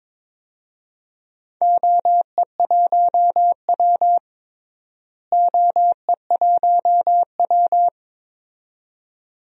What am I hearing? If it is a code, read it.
Morse code, heard as a keyed tone, about 21 wpm, 700 Hz. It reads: OE1W OE1W
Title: DXCC-100 Call Signs - 4 Characters - Repeated 2x - 22wpm